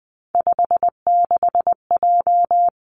Morse code, heard as a keyed tone, about 20 wpm, 700 Hz. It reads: 56J
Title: Three Challenging Letter-Number Combinations 20wpm